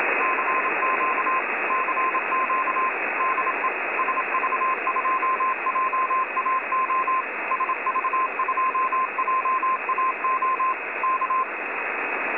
Morse code, heard as a keyed tone, about 26 wpm, 1040 Hz, with radio noise. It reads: CQ CQ DE SV2JAO SV2JAO K
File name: websdr recording 70175.0kHz